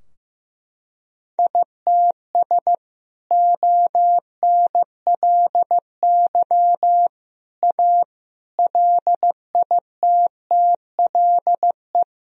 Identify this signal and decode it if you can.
Morse code, heard as a keyed tone, about 14 wpm, 700 Hz. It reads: ITS ONLY A LITTLE
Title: Sentences from Top 1000 Words - Review Part 4 - 15wpm